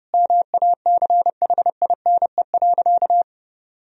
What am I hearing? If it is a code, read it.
Morse code, heard as a keyed tone, about 32 wpm, 700 Hz. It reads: MACHINE.
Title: Sentences from Top 1000 Words - Review Part 2 - 30wpm